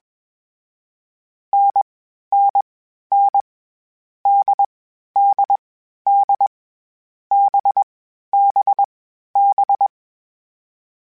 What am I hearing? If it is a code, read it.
Morse code, heard as a keyed tone, about 21 wpm, 800 Hz. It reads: N N N D D D B B B
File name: Difficult Combos / Difficult Combos-mp3s / NDB NDB6 @20